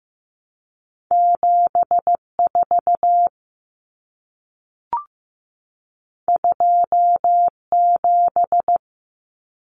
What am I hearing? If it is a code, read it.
Morse code, heard as a keyed tone, about 15 wpm, 700 Hz. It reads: 74 27
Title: Common Cut Numbers - 2-Digit 15wpm